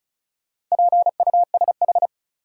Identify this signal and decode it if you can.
Morse code, heard as a keyed tone, about 35 wpm, 700 Hz. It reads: PUSH
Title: Sentences from Top 1300 Words - New Word Drill Alt - 35wpm